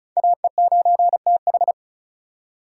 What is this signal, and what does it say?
Morse code, heard as a keyed tone, about 36 wpm, 700 Hz. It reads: AE9TH
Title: USA Call Signs - 5 Characters - 4x Character Spacing - 35wpm